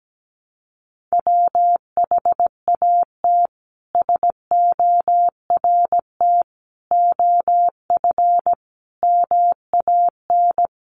Morse code, heard as a keyed tone, about 17 wpm, 700 Hz. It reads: WHAT SORT OF MAN
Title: Sentences from Top 1300 Words 17wpm